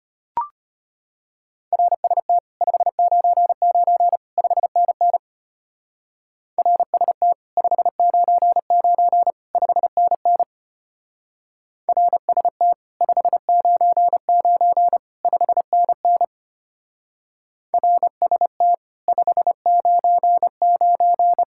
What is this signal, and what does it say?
Morse code, heard as a keyed tone, about 31 wpm, 700 Hz. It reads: RST 599 5NN RST 599 5NN RST 599 5NN RST 599
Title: Speed-Racing Common QSO Phrases - 25wpm